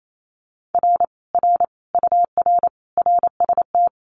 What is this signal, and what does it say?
Morse code, heard as a keyed tone, about 28 wpm, 700 Hz. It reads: R R UR RST
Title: Common QSO Phrases Encore 28wpm